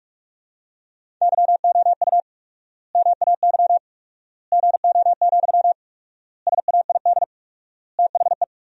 Morse code, heard as a keyed tone, about 42 wpm, 700 Hz. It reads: YOU MAY GO, SAID THE
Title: 4x Word-Spacing - Sentences from Top 400 Words - 45wpm